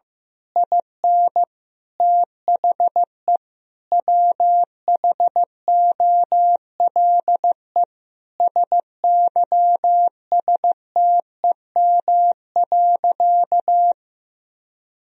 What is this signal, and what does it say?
Morse code, heard as a keyed tone, about 14 wpm, 700 Hz. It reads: IN THE WHOLE SYSTEM.
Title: Sentences from Top 1000 Words - Review Part 3 - 15wpm